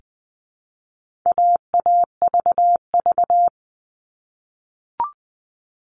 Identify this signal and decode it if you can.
Morse code, heard as a keyed tone, about 20 wpm, 700 Hz. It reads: AAVV
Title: All Cut Numbers - 4-Digit 20wpm